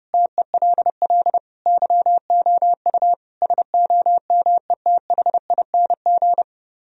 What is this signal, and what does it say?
Morse code, heard as a keyed tone, about 29 wpm, 700 Hz. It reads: TELL YOU SOMETHING